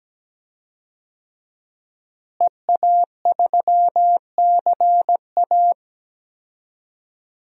Morse code, heard as a keyed tone, about 17 wpm, 700 Hz. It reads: EA3CA
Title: DXCC-100 Call Signs - 5 Characters - Repeated 3x - 17wpm